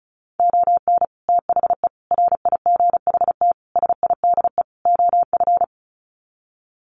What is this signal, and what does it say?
Morse code, heard as a keyed tone, about 34 wpm, 700 Hz. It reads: ON THE RIGHT SIDE OF
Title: Sets of 5 Words Encore 35wpm